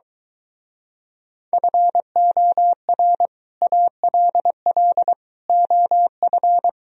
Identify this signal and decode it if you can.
Morse code, heard as a keyed tone, about 23 wpm, 700 Hz. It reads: FOR ALL OF